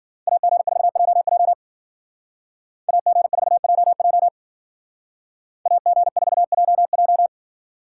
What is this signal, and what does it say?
Morse code, heard as a keyed tone, about 48 wpm, 700 Hz. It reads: AG4JJ AG4JJ AG4JJ
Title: Speed Racing - USA 2x2 Call Signs 40wpm